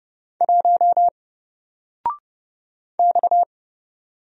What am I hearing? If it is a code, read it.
Morse code, heard as a keyed tone, about 29 wpm, 700 Hz. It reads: 1 X
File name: Single-Letter-Number-30wpm